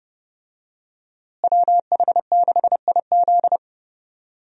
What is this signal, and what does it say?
Morse code, heard as a keyed tone, about 30 wpm, 700 Hz. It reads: WH6IZ
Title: USA Call Signs - 5 Characters - Repeated 2x - 30wpm